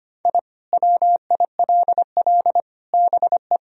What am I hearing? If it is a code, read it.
Morse code, heard as a keyed tone, about 25 wpm, 700 Hz. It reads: I WILL BE
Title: Sentences from Top 1000 Words - Review Part 2 - 25wpm